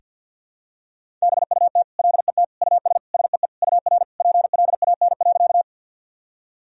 Morse code, heard as a keyed tone, about 48 wpm, 700 Hz. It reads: BUT LET US SEE UR PLAN.
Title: Sentences from Top 500 Words and Abbreviations 50wpm